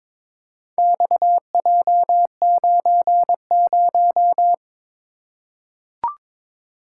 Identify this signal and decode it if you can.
Morse code, heard as a keyed tone, about 22 wpm, 700 Hz. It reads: XJ90
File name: practice / 22wpm / Four Letter-Number Combinations - 22wpm